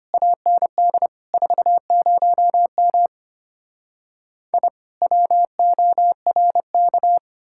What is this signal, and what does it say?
Morse code, heard as a keyed tone, about 33 wpm, 700 Hz. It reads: AND 40M I WORK